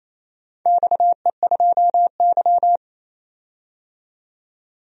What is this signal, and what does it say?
Morse code, heard as a keyed tone, about 27 wpm, 700 Hz. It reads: XE2Y
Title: DXCC-100 Call Signs - 4 Characters - 28wpm